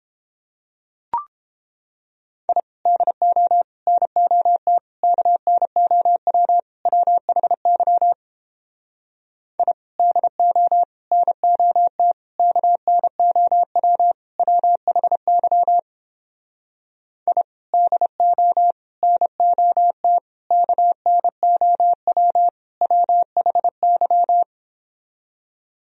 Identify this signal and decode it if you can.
Morse code, heard as a keyed tone, about 29 wpm, 700 Hz. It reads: I DO NOT KNOW WHY I DO NOT KNOW WHY I DO NOT KNOW WHY